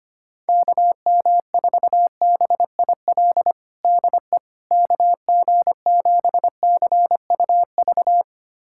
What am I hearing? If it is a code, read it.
Morse code, heard as a keyed tone, about 25 wpm, 700 Hz. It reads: KM4BIL DE KG7CUV
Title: USA Callsign DE Callsign - 6 Characters - 25wpm